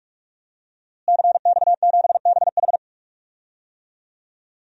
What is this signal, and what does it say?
Morse code, heard as a keyed tone, about 46 wpm, 700 Hz. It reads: CX7BH
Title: DXCC-100 Call Signs - 5 Characters - Repeated 2x - 45wpm